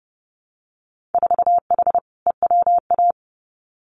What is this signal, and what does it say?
Morse code, heard as a keyed tone, about 30 wpm, 700 Hz. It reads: 4H EWA
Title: Winter Field Day Exchanges Encore 30wpm